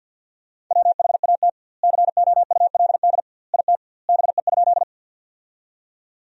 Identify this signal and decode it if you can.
Morse code, heard as a keyed tone, about 50 wpm, 700 Hz. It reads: WHAT COULD IT BE?
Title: 2x Word-Spacing - Sentences from Top 100 Words - 50wpm